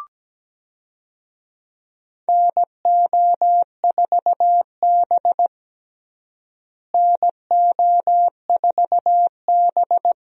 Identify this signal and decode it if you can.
Morse code, heard as a keyed tone, about 17 wpm, 700 Hz. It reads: NO4B NO4B
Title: USA Call Signs - 4 Characters - Repeated 2x - 17wpm